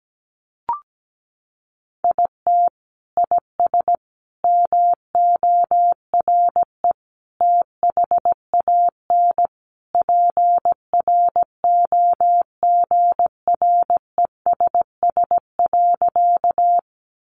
Morse code, heard as a keyed tone, about 17 wpm, 700 Hz. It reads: IT IS MORE THAN PROGRESS.
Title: Sentences from Top 1400 Words 17wpm